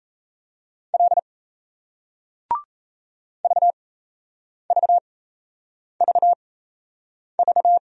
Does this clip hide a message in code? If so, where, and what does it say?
Yes: Morse code from 0.9 to 7.8 s, heard as a keyed tone, about 38 wpm, spaced as for 8 wpm, 700 Hz. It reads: L VVVV